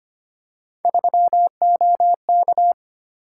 Morse code, heard as a keyed tone, about 25 wpm, 700 Hz. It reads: 3OK
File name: Three Letter-Number Combinations - 25wpm